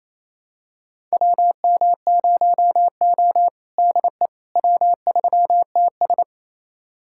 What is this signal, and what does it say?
Morse code, heard as a keyed tone, about 28 wpm, 700 Hz. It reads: WM0O DE W3TS
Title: USA Callsign DE Callsign - 4 Characters - 28wpm